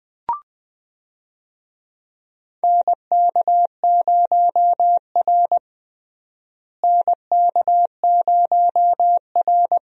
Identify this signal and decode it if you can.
Morse code, heard as a keyed tone, about 20 wpm, 700 Hz. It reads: NK0R NK0R
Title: USA Call Signs - 4 Characters - Repeated 2x - 20wpm